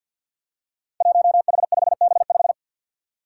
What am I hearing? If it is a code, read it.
Morse code, heard as a keyed tone, about 53 wpm, 700 Hz. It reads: 1H5B5
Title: Five Challenging Letter-Number Combinations 50wpm